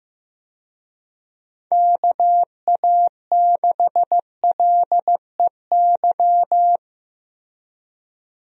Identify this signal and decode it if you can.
Morse code, heard as a keyed tone, about 15 wpm, 700 Hz. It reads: KA6LEY
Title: USA Call Signs - 6 Characters - Repeated 2x - 15wpm